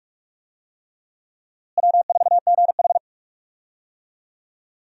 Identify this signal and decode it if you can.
Morse code, heard as a keyed tone, about 46 wpm, 700 Hz. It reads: W4GH
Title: USA Call Signs - 4 Characters - Repeated 3x - 45wpm